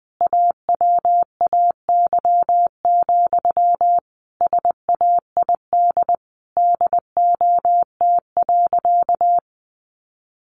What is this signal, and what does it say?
Morse code, heard as a keyed tone, about 20 wpm, 700 Hz. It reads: AWAY, SAID DOT.
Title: Sentences from Top 1500 Words 20wpm